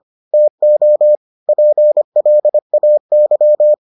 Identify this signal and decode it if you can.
Morse code, heard as a keyed tone, about 25 wpm, 600 Hz. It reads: TO PLAY